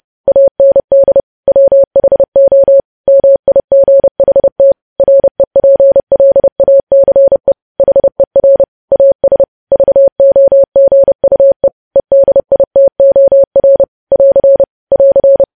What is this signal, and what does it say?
Morse code, heard as a keyed tone, about 30 wpm, 550 Hz. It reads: AND WHO MIGHT REPLACE HER AS VOGUE EDITOR <AR> <AR>
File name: podcast / News headlines-1x-30x30